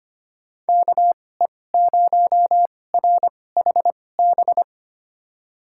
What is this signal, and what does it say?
Morse code, heard as a keyed tone, about 25 wpm, spaced as for 18 wpm, 700 Hz. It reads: KE0RHB
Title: USA Call Signs - 2x Character Spacing - 25wpm